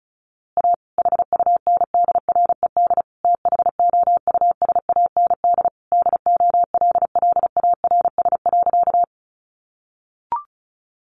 Morse code, heard as a keyed tone, about 34 wpm, 700 Hz. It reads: A HUNDRED THOUSAND DOLLARS.
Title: A Sound Of Thunder 35wpm